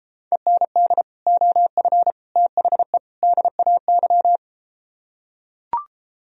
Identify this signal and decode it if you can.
Morse code, heard as a keyed tone, about 33 wpm, 700 Hz. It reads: END OF THE DAY